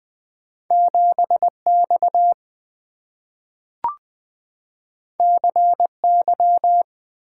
Morse code, heard as a keyed tone, about 20 wpm, 700 Hz. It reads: 7X CY
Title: Two Letter-Number Combinations - 20wpm